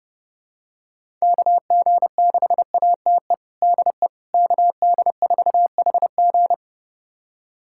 Morse code, heard as a keyed tone, about 32 wpm, 700 Hz. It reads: KG6ATE DE KD4HG
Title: USA Callsign DE Callsign 30wpm